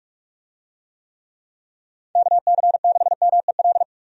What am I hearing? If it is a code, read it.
Morse code, heard as a keyed tone, about 46 wpm, 700 Hz. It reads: KC6MEL
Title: USA Call Signs - 6 Characters - 45wpm